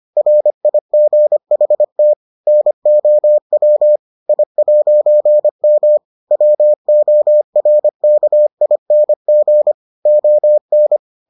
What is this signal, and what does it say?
Morse code, heard as a keyed tone, about 25 wpm, 600 Hz. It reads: RIGHT NOW I'M WORKING ON